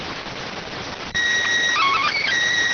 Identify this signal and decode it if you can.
SSTV (slow-scan television), PD 290.